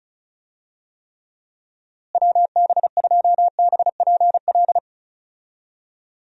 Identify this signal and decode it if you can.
Morse code, heard as a keyed tone, about 34 wpm, 700 Hz. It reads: WB2BPL
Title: USA Call Signs - 6 Characters - Repeated 3x - 35wpm